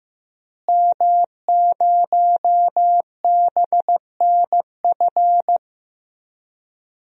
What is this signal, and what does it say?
Morse code, heard as a keyed tone, about 15 wpm, 700 Hz. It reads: M0BNF
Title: DXCC-100 Call Signs - 5 Characters - Repeated 2x - 15wpm